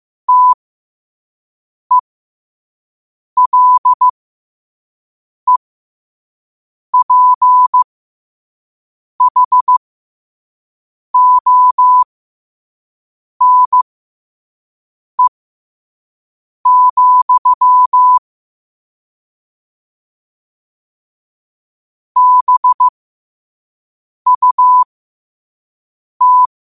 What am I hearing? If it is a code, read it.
Morse code, heard as a keyed tone, about 15 wpm, spaced as for 5 wpm, 1000 Hz. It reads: TELEPHONE, BUT
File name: qotdfiles / 5wpm